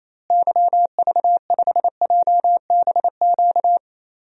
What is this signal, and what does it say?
Morse code, heard as a keyed tone, about 28 wpm, 700 Hz. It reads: YV5JBQ
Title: DXCC-100 Call Signs - 6 Characters - Repeated 2x - 28wpm